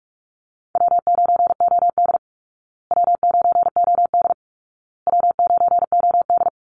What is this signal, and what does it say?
Morse code, heard as a keyed tone, about 44 wpm, 700 Hz. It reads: W9OD W9OD W9OD
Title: Easy Call Signs Repeated 3x - 45wpm